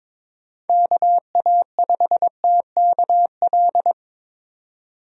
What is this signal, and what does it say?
Morse code, heard as a keyed tone, about 23 wpm, 700 Hz. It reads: KA5TKL
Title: USA Call Signs - 6 Characters - 22wpm